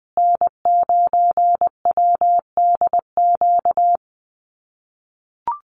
Morse code, heard as a keyed tone, about 20 wpm, 700 Hz. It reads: N9WDQ
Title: USA Call Signs Repeated 2x 20wpm